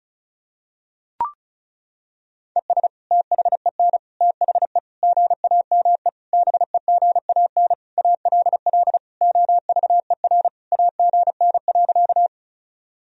Morse code, heard as a keyed tone, about 34 wpm, 700 Hz. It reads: ES THEN THE GAME BEGAN ALL OVER AGN.
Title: Sentences from Top 500 Words and Abbreviations 35wpm